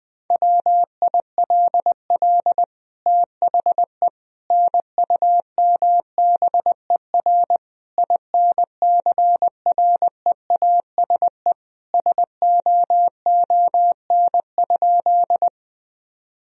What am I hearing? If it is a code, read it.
Morse code, heard as a keyed tone, about 21 wpm, 700 Hz. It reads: WILL THE NUMBER INCREASE SOON?
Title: Sentences from Top 1000 Words - Review Part 4 - 20wpm